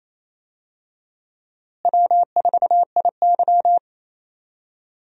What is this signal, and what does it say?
Morse code, heard as a keyed tone, about 28 wpm, 700 Hz. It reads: W4IY
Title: USA Call Signs - 4 Characters - Repeated 3x - 28wpm